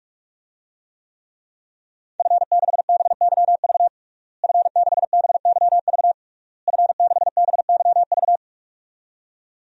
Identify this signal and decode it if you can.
Morse code, heard as a keyed tone, about 48 wpm, 700 Hz. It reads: F6BYV F6BYV F6BYV
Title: DXCC-100 Call Signs - 5 Characters - Repeated 3x - 45wpm